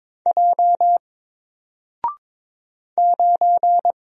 Morse code, heard as a keyed tone, about 23 wpm, 700 Hz. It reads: J 9
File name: Single-Letter-Number-22wpm